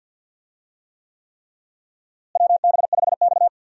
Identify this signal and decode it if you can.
Morse code, heard as a keyed tone, about 52 wpm, 700 Hz. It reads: WB5X
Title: USA Call Signs - 4 Characters - Repeated 3x - 50wpm